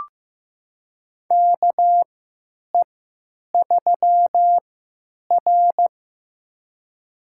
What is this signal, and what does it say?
Morse code, heard as a keyed tone, about 15 wpm, spaced as for 9 wpm, 700 Hz. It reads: KE3R
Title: USA Call Signs - 4 Characters - 3x Character Spacing - 15wpm